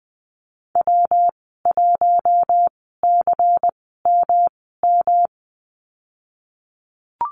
Morse code, heard as a keyed tone, about 20 wpm, spaced as for 15 wpm, 700 Hz. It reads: W1CMM
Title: 122 Lesson 20g - Number 3 - Call Signs 20wpm